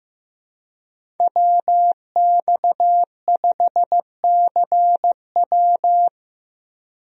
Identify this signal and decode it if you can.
Morse code, heard as a keyed tone, about 15 wpm, 700 Hz. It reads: WX5CW
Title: USA Call Signs - 5 Characters - Repeated 2x - 15wpm